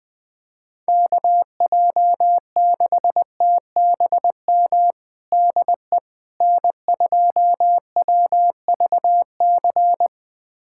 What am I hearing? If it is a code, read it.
Morse code, heard as a keyed tone, about 20 wpm, 700 Hz. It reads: KJ6TBM DE N2WVC
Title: USA Callsign DE Callsign 20wpm